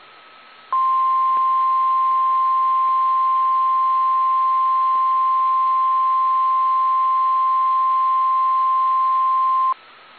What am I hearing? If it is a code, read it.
Emergency Alert System (SAME) tones (the 1050 Hz attention tone). It is heard from 0.7 to 9.8 s.
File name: eas / KSDZKDJL / KSDZKDJL